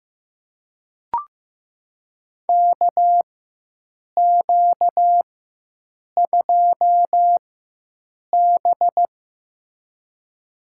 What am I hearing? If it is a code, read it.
Morse code, heard as a keyed tone, about 15 wpm, spaced as for 7 wpm, 700 Hz. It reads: KQ2B
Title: USA Call Signs - 4 Characters - 4x Character Spacing - 15wpm